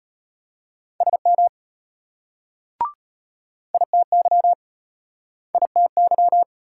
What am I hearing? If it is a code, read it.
Morse code, heard as a keyed tone, about 34 wpm, 700 Hz. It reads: SM ITY ITY